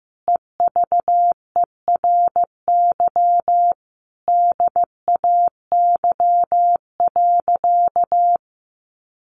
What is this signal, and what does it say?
Morse code, heard as a keyed tone, about 15 wpm, 700 Hz. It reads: EVERY DAY.
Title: Sentences from Top 1300 Words 15wpm